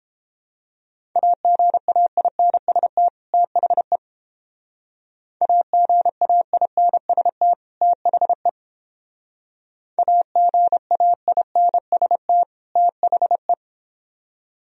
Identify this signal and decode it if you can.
Morse code, heard as a keyed tone, about 31 wpm, 700 Hz. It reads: AGAINST THE AGAINST THE AGAINST THE